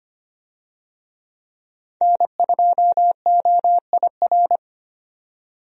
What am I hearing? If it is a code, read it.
Morse code, heard as a keyed tone, about 25 wpm, 700 Hz. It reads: N2OIR
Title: USA Call Signs - 5 Characters - 25wpm